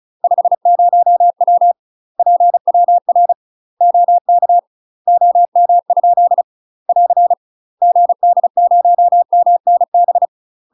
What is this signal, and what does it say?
Morse code, heard as a keyed tone, about 34 wpm, 700 Hz. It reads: 50W PWR OK OM? <AR> GD0MNB
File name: UK MASTER QSOs TRAINING / 35WPM / UK MASTER QSOs 0037